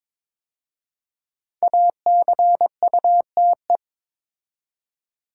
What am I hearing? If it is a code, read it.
Morse code, heard as a keyed tone, about 22 wpm, 700 Hz. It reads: ACUTE
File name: 5 Letter Words - First 3 Letters Given - 22wpm